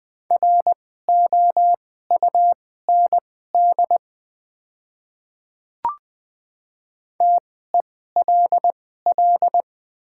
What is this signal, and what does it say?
Morse code, heard as a keyed tone, about 20 wpm, spaced as for 15 wpm, 700 Hz. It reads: ROUND TELL
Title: 2x Character Spacing - Top 200 Words - 20wpm